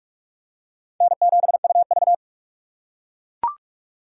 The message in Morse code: N7UV

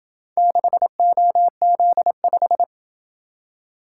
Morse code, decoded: 6OZ5

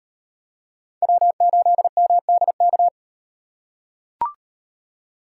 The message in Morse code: W8MDK

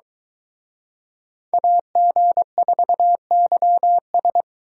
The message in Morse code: AG4YS

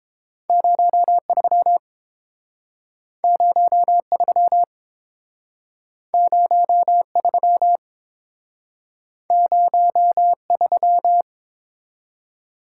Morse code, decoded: 03 03 03 03

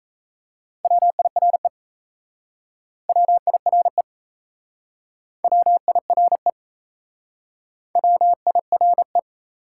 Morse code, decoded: WIRE WIRE WIRE WIRE

